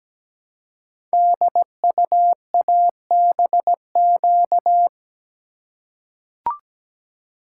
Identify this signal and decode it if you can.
Morse code, heard as a keyed tone, about 17 wpm, 700 Hz. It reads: DUABQ
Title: Code Groups - 5 Letters - 17wpm